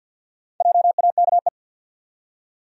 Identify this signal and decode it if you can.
Morse code, heard as a keyed tone, about 52 wpm, 700 Hz. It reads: JAKE
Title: USA Common Names - Max 5 Letters - 50wpm